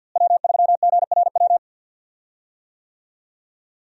Morse code, heard as a keyed tone, about 50 wpm, 700 Hz. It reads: W3GRW
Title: USA Call Signs - 5 Characters - Repeated 2x - 50wpm